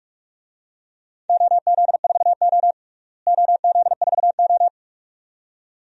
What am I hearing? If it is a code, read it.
Morse code, heard as a keyed tone, about 43 wpm, 700 Hz. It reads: OZ4O OZ4O
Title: DXCC-100 Call Signs - 4 Characters - Repeated 2x - 45wpm